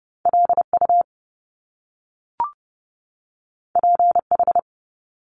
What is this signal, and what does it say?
Morse code, heard as a keyed tone, about 30 wpm, 700 Hz. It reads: LU PH
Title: Top Two Letter Combinations in Words 30wpm